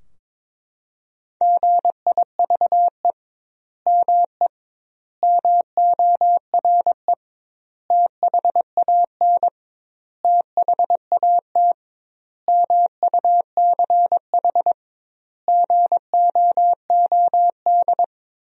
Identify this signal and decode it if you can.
Morse code, heard as a keyed tone, about 22 wpm, 700 Hz. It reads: GIVE ME MORE THAN THAT MUCH GOOD
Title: 2x Word-Spacing - Sentences from Top 200 Words - 22wpm